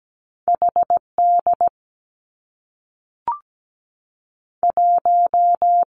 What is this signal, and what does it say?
Morse code, heard as a keyed tone, about 17 wpm, 700 Hz. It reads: H D 1
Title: Two Challenging Letter-Number Combinations 17wpm